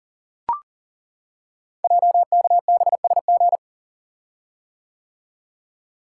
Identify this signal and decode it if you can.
Morse code, heard as a keyed tone, about 38 wpm, 700 Hz. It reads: JKBSG
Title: Code Groups - 5 Letters - 40wpm